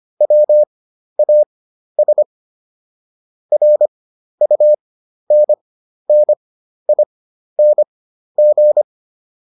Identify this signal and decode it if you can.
Morse code, heard as a keyed tone, about 25 wpm, spaced as for 12 wpm, 600 Hz. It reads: WAS RUNNING